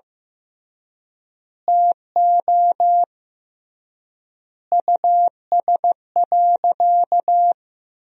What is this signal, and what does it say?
Morse code, heard as a keyed tone, about 15 wpm, 700 Hz. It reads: TO US.
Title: 3x Word-Spacing - Sentences from Top 200 Words - 15wpm